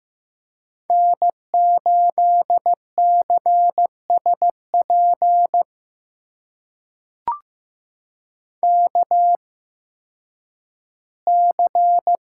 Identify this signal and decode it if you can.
Morse code, heard as a keyed tone, about 15 wpm, spaced as for 10 wpm, 700 Hz. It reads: N8CSP KC
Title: USA Call Signs - 8x Character Spacing - 15wpm